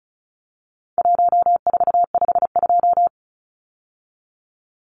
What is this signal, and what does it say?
Morse code, heard as a keyed tone, about 34 wpm, 700 Hz. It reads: 1452
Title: Numbers - Contest Style - 4-Digit 35wpm